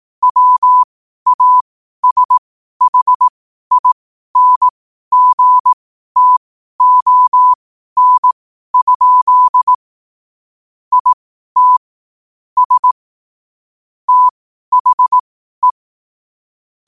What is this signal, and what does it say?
Morse code, heard as a keyed tone, about 18 wpm, spaced as for 12 wpm, 1000 Hz. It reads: WASHINGTON? IT S THE
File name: qotdfiles / 12wpm